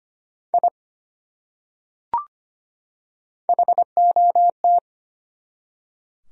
Morse code, heard as a keyed tone, about 25 wpm, 700 Hz. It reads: I HOT
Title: Top 100 Words 25wpm